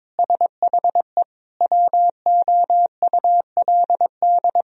Morse code, heard as a keyed tone, about 22 wpm, 700 Hz. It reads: SHE WOULD
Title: Sentences from Top 1000 Words - Review Part 4 - 22wpm